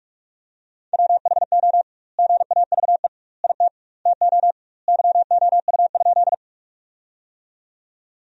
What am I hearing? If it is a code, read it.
Morse code, heard as a keyed tone, about 44 wpm, 700 Hz. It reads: WHO GAVE IT TO YOU?
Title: 2x Word-Spacing - Sentences from Top 500 Words - 45wpm